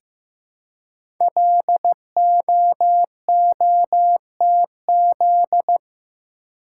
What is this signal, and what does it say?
Morse code, heard as a keyed tone, about 15 wpm, 700 Hz. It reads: LOOTZ
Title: Code Groups - 5 Letters - 15wpm